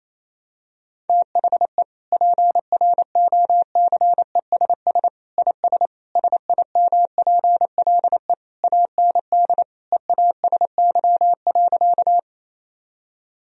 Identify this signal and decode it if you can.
Morse code, heard as a keyed tone, about 28 wpm, 700 Hz. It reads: THE PROCESS IS SIMPLE AND EASY.
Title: Sentences from Top 1000 Words - Review Part 3 - 28wpm